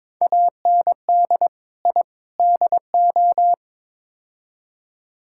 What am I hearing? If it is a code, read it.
Morse code, heard as a keyed tone, about 22 wpm, 700 Hz. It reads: AND I DO
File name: practice / 22wpm / Build A Phrase - 5-Word Phrases - 22wpm